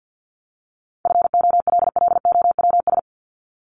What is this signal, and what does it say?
Morse code, heard as a keyed tone, about 48 wpm, 700 Hz. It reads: FOLLOWS
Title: ES:S Suffix - Top 100 Words with Base Words 50wpm